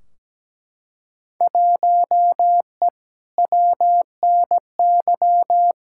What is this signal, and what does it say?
Morse code, heard as a keyed tone, about 17 wpm, 700 Hz. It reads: 1E WNY
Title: Field Day Exchanges - 17wpm